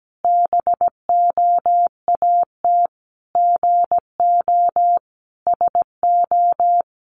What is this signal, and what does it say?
Morse code, heard as a keyed tone, about 17 wpm, 700 Hz. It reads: BOAT GO SO
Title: Sentences from Top 500 Words 17wpm